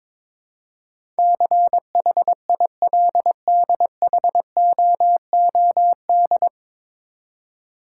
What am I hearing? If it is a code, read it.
Morse code, heard as a keyed tone, about 22 wpm, 700 Hz. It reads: CHILDHOOD
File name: 9 Letter Words - First 3 Letters Given - 22wpm